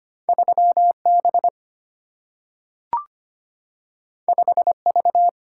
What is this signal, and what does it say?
Morse code, heard as a keyed tone, about 25 wpm, 700 Hz. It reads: 3 B 5 V
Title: Single-Letter Single-Number 25wpm